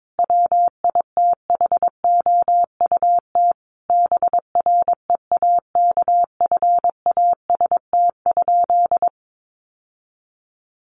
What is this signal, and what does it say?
Morse code, heard as a keyed tone, about 22 wpm, 700 Hz. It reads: WITHOUT BREAKFAST?
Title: Sentences from Top 1500 Words 22wpm